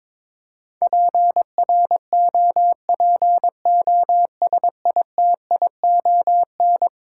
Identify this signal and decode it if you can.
Morse code, heard as a keyed tone, about 21 wpm, 700 Hz. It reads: PROPOSITION
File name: practice / 22wpm / Eleven Letter Words - 22wpm